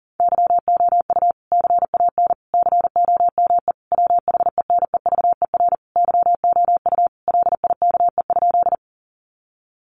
Morse code, heard as a keyed tone, about 37 wpm, 700 Hz. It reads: YOU CAN COME WHENEVER YOU LIKE?
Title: Sentences from Top 1200 Words 40wpm